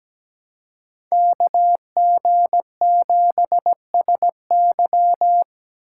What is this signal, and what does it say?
Morse code, heard as a keyed tone, about 20 wpm, 700 Hz. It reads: KG7SY
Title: Speed Racing - USA 2x2 Call Signs 17wpm